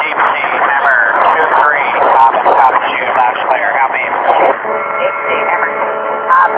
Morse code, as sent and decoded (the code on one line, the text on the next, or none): none